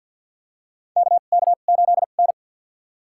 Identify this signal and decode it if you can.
Morse code, heard as a keyed tone, about 52 wpm, spaced as for 37 wpm, 700 Hz. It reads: KK8N
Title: USA Call Signs - 4 Characters - 2x Character Spacing - 50wpm